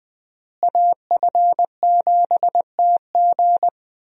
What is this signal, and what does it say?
Morse code, heard as a keyed tone, about 21 wpm, 700 Hz. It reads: AF7TG
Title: USA Call Signs - 5 Characters - Repeated 3x - 20wpm